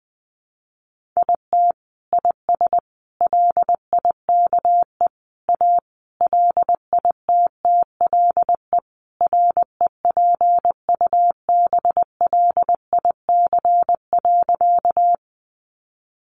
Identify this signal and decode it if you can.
Morse code, heard as a keyed tone, about 20 wpm, 700 Hz. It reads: IT IS LIKE A LITTLE REPUBLIC.
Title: Sentences from Top 1500 Words 20wpm